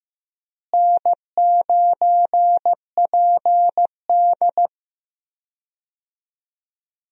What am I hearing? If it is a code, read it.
Morse code, heard as a keyed tone, about 15 wpm, 700 Hz. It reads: N9PD
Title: USA Call Signs - 4 Characters - Repeated 3x - 15wpm